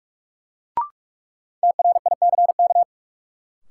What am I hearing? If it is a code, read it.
Morse code, heard as a keyed tone, about 46 wpm, 700 Hz. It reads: TRICK
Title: Sentences from Top 1500 Words - New Word Drill Alt - 45wpm